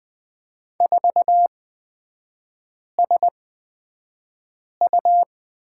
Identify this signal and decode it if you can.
Morse code, heard as a keyed tone, about 20 wpm, spaced as for 5 wpm, 700 Hz. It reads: 4SU
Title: 255 Lesson 41e - Period - Three Characters 20wpm